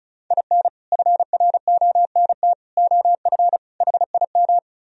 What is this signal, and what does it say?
Morse code, heard as a keyed tone, about 43 wpm, 700 Hz. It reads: IN FRONT OF HIM
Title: Speed-Racing - Sets of 4 Words 35wpm